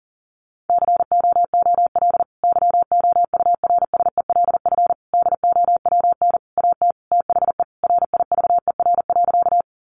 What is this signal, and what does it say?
Morse code, heard as a keyed tone, about 40 wpm, 700 Hz. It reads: COOL YOURSELF DOWN AT THE RIVER.
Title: Sentences from Top 1200 Words 40wpm